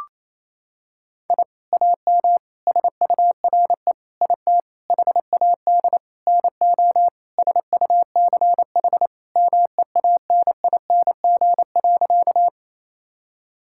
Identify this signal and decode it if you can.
Morse code, heard as a keyed tone, about 28 wpm, 700 Hz. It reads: I AM SURE IT HAD NO SUCH MEANING.
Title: Sentences from Top 1000 Words - Review Part 6 - 28wpm